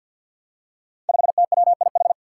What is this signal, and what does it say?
Morse code, heard as a keyed tone, about 52 wpm, 700 Hz. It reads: 5TWIH
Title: Five Challenging Letter-Number Combinations 50wpm